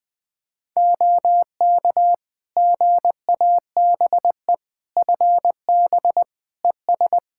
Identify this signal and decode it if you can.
Morse code, heard as a keyed tone, about 21 wpm, 700 Hz. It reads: OK GABE FB ES
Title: Practice QSO - Exchange 2 - Info - 20wpm